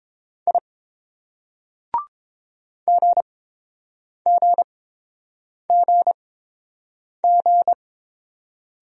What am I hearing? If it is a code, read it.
Morse code, heard as a keyed tone, about 28 wpm, spaced as for 7 wpm, 690 Hz. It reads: I GGGG